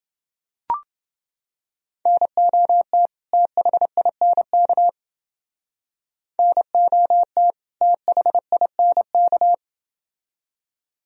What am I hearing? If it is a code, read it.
Morse code, heard as a keyed tone, about 28 wpm, 700 Hz. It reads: NOT THINK NOT THINK